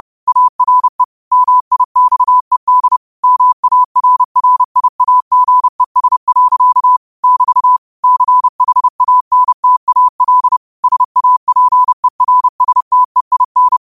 Morse code, heard as a keyed tone, about 30 wpm, 1000 Hz. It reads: ARE MIXED MARRIAGES. <BT> CHANTAL SAPERSTEIN